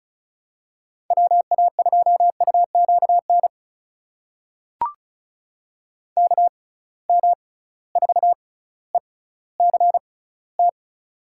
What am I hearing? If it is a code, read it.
Morse code, heard as a keyed tone, about 36 wpm, spaced as for 18 wpm, 700 Hz. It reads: WA2UQN KM4ECT
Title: USA Call Signs - 6 Characters - 6x Character Spacing - 35wpm